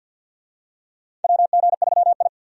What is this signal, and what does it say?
Morse code, heard as a keyed tone, about 52 wpm, 700 Hz. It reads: WG3I
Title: USA Call Signs - 4 Characters - 50wpm